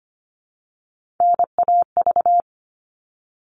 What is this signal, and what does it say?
Morse code, heard as a keyed tone, about 25 wpm, 700 Hz. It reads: NAV
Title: All Cut Numbers - 3-Digit 25wpm